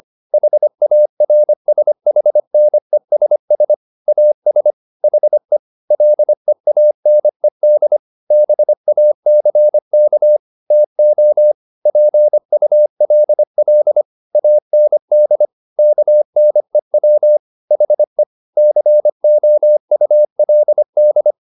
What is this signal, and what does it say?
Morse code, heard as a keyed tone, about 25 wpm, 600 Hz. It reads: HARSHNESS AS HE LEANED BACK TO PULL AND KNEW HE COULD
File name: the old man and the sea part 1 25wpm